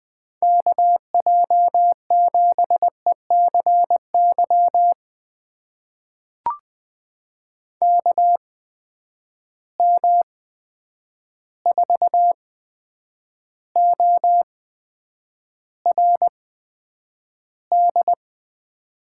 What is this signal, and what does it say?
Morse code, heard as a keyed tone, about 21 wpm, spaced as for 9 wpm, 700 Hz. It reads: KJ7ECY KM4ORD
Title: USA Call Signs - 6 Characters - 8x Character Spacing - 20wpm